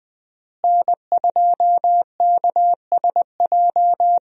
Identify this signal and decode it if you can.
Morse code, heard as a keyed tone, about 20 wpm, 700 Hz. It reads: N2KSJ
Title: USA Call Signs - 5 Characters - Repeated 3x - 20wpm